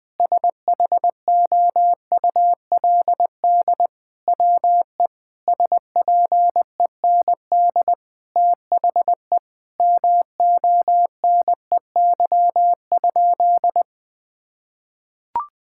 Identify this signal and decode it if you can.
Morse code, heard as a keyed tone, about 20 wpm, 700 Hz. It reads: SHOULD WE SPEND THE MONEY?
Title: Sentences from Top 1000 Words - Review Part 5 - 20wpm